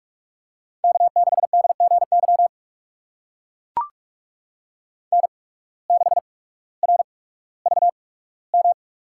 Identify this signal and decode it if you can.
Morse code, heard as a keyed tone, about 44 wpm, spaced as for 19 wpm, 700 Hz. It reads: K6DGY N6RVM
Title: USA Call Signs - 5 Characters - 8x Character Spacing - 45wpm